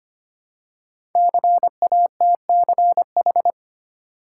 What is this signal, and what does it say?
Morse code, heard as a keyed tone, about 25 wpm, 700 Hz. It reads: CATCH
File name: 5 Letter Words - First 3 Letters Given - 25wpm